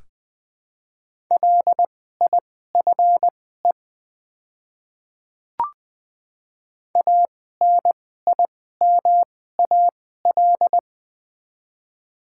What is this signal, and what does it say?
Morse code, heard as a keyed tone, about 20 wpm, spaced as for 14 wpm, 690 Hz. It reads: LIFE ANIMAL
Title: 2x Character Spacing - Top 500 Words - 20wpm